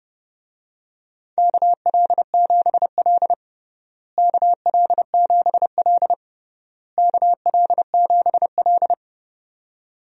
Morse code, heard as a keyed tone, about 30 wpm, 700 Hz. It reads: KL7L KL7L KL7L
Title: DXCC-100 Call Signs - 4 Characters - Repeated 3x - 30wpm